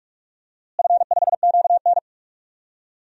Morse code, heard as a keyed tone, about 46 wpm, 700 Hz. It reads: F5QN
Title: DXCC-100 Call Signs - 4 Characters - Repeated 3x - 45wpm